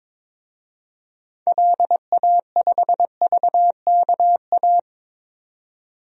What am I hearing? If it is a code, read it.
Morse code, heard as a keyed tone, about 22 wpm, 700 Hz. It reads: LA5VKA
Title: DXCC-100 Call Signs - 6 Characters - 22wpm